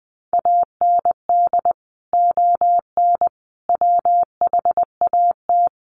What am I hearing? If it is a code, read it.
Morse code, heard as a keyed tone, about 20 wpm, 700 Hz. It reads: AND ON WHAT
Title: Sentences from Top 1100 Words 20wpm